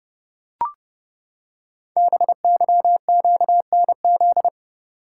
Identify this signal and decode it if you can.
Morse code, heard as a keyed tone, about 30 wpm, 700 Hz. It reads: BYQNZ
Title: Code Groups - 5 Letters - 30wpm